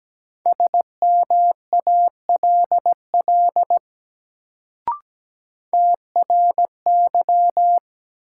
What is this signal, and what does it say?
Morse code, heard as a keyed tone, about 17 wpm, 700 Hz. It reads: SMALL TRY
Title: Sentences from Top 200 Words - New Word Drill - 17wpm